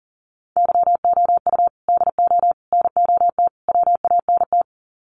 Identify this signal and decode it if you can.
Morse code, heard as a keyed tone, about 40 wpm, 700 Hz. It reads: YOU DO NOT WANT
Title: Sets of 4 Words 40wpm